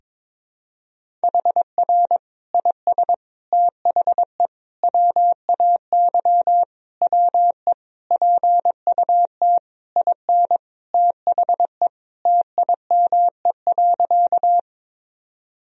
Morse code, heard as a keyed tone, about 22 wpm, 700 Hz. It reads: HR IS THE WAY WE PUT IN THE TIME.
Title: Sentences from Top 200 Words and Abbreviations 22wpm